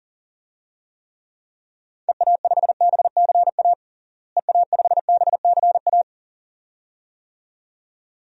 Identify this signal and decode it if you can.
Morse code, heard as a keyed tone, about 40 wpm, 700 Hz. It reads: EA5BCA EA5BCA
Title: DXCC-100 Call Signs - 6 Characters - Repeated 2x - 40wpm